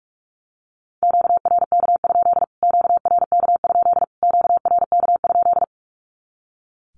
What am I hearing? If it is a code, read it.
Morse code, heard as a keyed tone, about 44 wpm, 700 Hz. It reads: QRK? QRK? QRK?
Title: Q-Codes Repeated 3x 45wpm